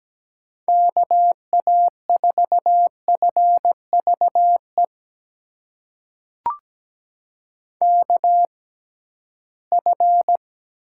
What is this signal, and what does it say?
Morse code, heard as a keyed tone, about 17 wpm, spaced as for 13 wpm, 700 Hz. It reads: KA4FVE KF
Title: USA Call Signs - 6 Characters - 6x Character Spacing - 17wpm